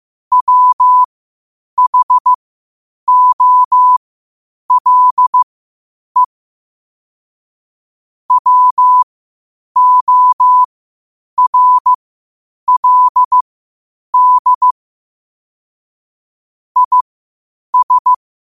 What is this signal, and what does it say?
Morse code, heard as a keyed tone, about 15 wpm, spaced as for 8 wpm, 1000 Hz. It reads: WHOLE WORLD IS
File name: qotdfiles / 8wpm